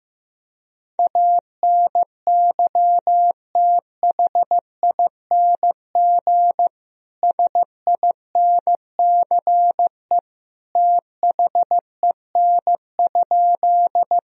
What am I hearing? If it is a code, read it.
Morse code, heard as a keyed tone, about 15 wpm, 700 Hz. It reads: ANYTHING SINCE THEN?
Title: Sentences from Top 1000 Words - Review Part 1 - 15wpm